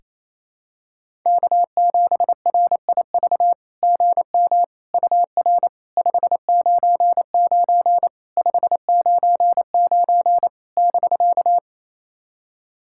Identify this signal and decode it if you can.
Morse code, heard as a keyed tone, about 28 wpm, 700 Hz. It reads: K7RIV GM UR 599 599 <BK>
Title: SOTA QSO Practice 28wpm